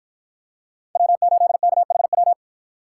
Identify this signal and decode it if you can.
Morse code, heard as a keyed tone, about 53 wpm, 700 Hz. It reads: W8KHW